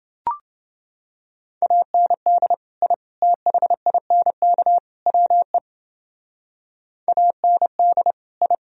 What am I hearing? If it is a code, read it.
Morse code, heard as a keyed tone, about 29 wpm, 700 Hz. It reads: AND I THINK WE AND I